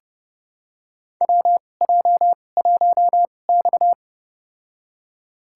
Morse code, heard as a keyed tone, about 31 wpm, spaced as for 22 wpm, 700 Hz. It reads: WJ1X
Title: USA Call Signs - 4 Characters - 2x Character Spacing - 30wpm